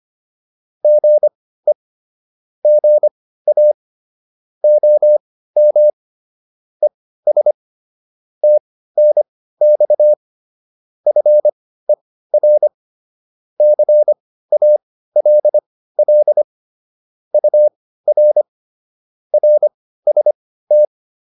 Morse code, heard as a keyed tone, about 25 wpm, spaced as for 15 wpm, 600 Hz. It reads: GE GA OM ES TNX FER CALL UR RST